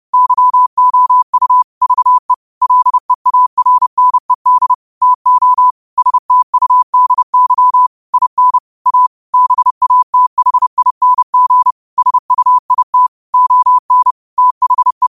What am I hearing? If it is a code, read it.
Morse code, heard as a keyed tone, about 30 wpm, 1000 Hz. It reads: YOU VE LEARNED TO STUDY IN A BATHING SUIT ON THE
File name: qotdfiles / 30wpm